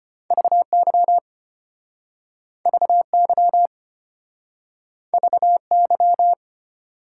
Morse code, heard as a keyed tone, about 33 wpm, 700 Hz. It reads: VY VY VY